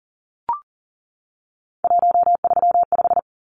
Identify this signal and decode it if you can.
Morse code, heard as a keyed tone, about 40 wpm, 700 Hz. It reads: 135
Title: Numbers - 3-Digit 40wpm